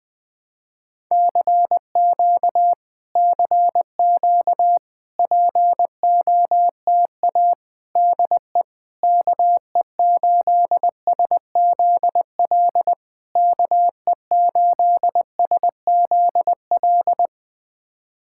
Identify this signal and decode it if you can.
Morse code, heard as a keyed tone, about 20 wpm, 700 Hz. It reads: CQ CQ POTA DE KE8SZL KE8SZL
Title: POTA QSO Practice - 20wpm